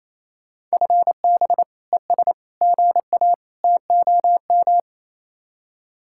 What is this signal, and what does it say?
Morse code, heard as a keyed tone, about 35 wpm, 700 Hz. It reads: FB ES GA TOM
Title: Speed-Racing Common QSO Phrases - 28wpm